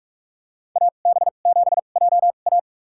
Morse code, heard as a keyed tone, about 46 wpm, spaced as for 33 wpm, 700 Hz. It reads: AB7JA
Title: USA Call Signs - 5 Characters - 2x Character Spacing - 45wpm